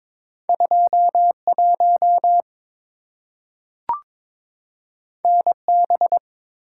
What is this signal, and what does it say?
Morse code, heard as a keyed tone, about 22 wpm, 700 Hz. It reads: 21 NB
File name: practice / 22wpm / Two Letter-Number Combinations - 22wpm